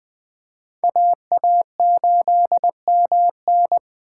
Morse code, heard as a keyed tone, about 20 wpm, 700 Hz. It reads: AA8MN
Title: Speed Racing - USA 2x2 Call Signs 17wpm